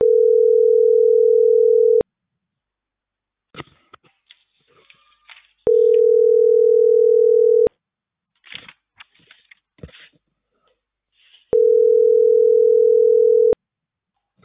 Ringback tone (North America), 440 + 480 Hz.